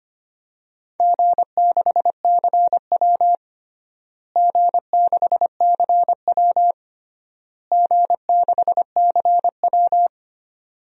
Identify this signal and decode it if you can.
Morse code, heard as a keyed tone, about 25 wpm, 700 Hz. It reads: G6CW G6CW G6CW
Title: DXCC-100 Call Signs - 4 Characters - Repeated 3x - 25wpm